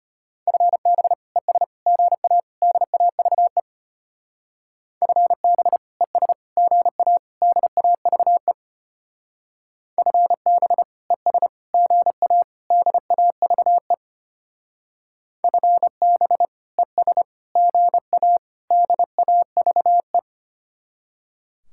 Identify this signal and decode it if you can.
Morse code, heard as a keyed tone, about 32 wpm, 700 Hz. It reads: FB ES GA DAVE FB ES GA DAVE FB ES GA DAVE FB ES GA DAVE